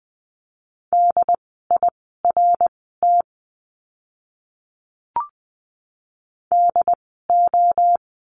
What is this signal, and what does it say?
Morse code, heard as a keyed tone, about 20 wpm, spaced as for 15 wpm, 700 Hz. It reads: DIRT DO
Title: 068 Lesson 12f - Letter d - Words 20wpm